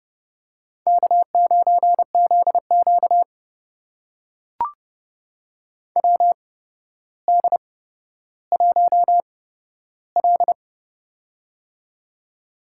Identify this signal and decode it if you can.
Morse code, heard as a keyed tone, about 30 wpm, spaced as for 13 wpm, 700 Hz. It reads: K9ZQ WD1L
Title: USA Call Signs - 4 Characters - 8x Character Spacing - 30wpm